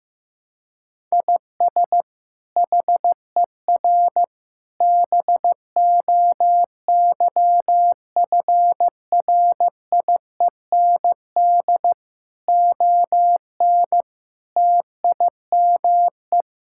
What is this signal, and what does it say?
Morse code, heard as a keyed tone, about 15 wpm, 700 Hz. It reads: IS HER BOYFRIEND ON TIME
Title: Sentences from Top 1000 Words - Review Part 5 - 15wpm